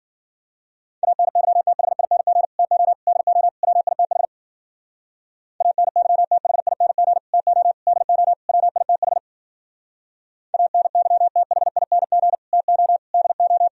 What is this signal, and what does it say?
Morse code, heard as a keyed tone, about 51 wpm, 700 Hz. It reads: ANYTHING TO DO WITH ANYTHING TO DO WITH ANYTHING TO DO